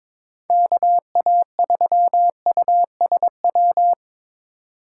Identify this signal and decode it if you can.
Morse code, heard as a keyed tone, about 22 wpm, 700 Hz. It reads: KA3USW
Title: USA Call Signs - 6 Characters - 2x Character Spacing - 22wpm